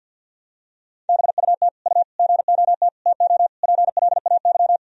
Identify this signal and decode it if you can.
Morse code, heard as a keyed tone, about 46 wpm, 700 Hz. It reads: BUT U GOT TO PLAY
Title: Sentences from Top 300 Words and Abbreviations 50wpm